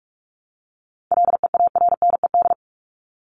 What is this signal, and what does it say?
Morse code, heard as a keyed tone, about 46 wpm, 700 Hz. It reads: LEARNED
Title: ED Suffix - Top 100 Words with Base Words 45wpm